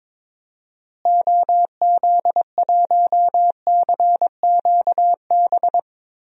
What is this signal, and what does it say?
Morse code, heard as a keyed tone, about 22 wpm, 700 Hz. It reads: OZ1CQB
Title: DXCC-100 Call Signs - 6 Characters - Repeated 3x - 22wpm